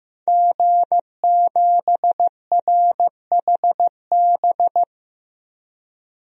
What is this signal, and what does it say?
Morse code, heard as a keyed tone, about 15 wpm, 700 Hz. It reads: G7RHB
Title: DXCC-100 Call Signs - 5 Characters - 15wpm